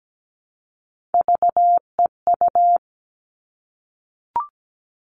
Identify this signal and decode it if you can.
Morse code, heard as a keyed tone, about 17 wpm, 700 Hz. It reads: VEU
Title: All Cut Numbers - 3-Digit 17wpm